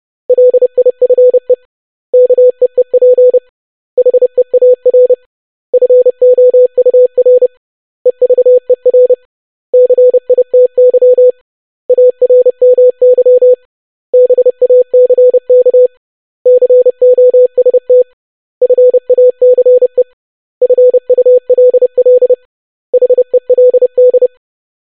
Morse code, heard as a keyed tone, about 30 wpm, 500 Hz. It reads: LIFE KEEP HEAR FOUR EVER CITY ARMY BACK COST FACE FULL HELD
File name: basic500-25wpm